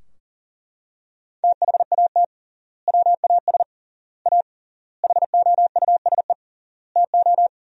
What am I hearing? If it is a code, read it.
Morse code, heard as a keyed tone, about 40 wpm, 700 Hz. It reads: THAT WAS A HOUSE TO
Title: 3x Word-Spacing - Sentences from Top 200 Words - 40wpm